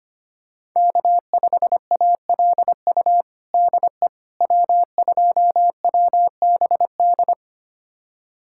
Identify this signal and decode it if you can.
Morse code, heard as a keyed tone, about 25 wpm, 700 Hz. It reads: K5ALU DE W2WBD
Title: USA Callsign DE Callsign - 5 Characters - 25wpm